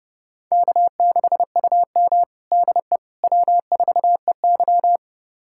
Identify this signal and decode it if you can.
Morse code, heard as a keyed tone, about 30 wpm, 700 Hz. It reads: K6UM DE W4EY
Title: USA Callsign DE Callsign - 4 Characters - 30wpm